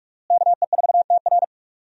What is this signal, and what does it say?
Morse code, heard as a keyed tone, about 41 wpm, 700 Hz. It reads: KE4TR
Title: USA Call Signs - 5 Characters - Repeated 3x - 45wpm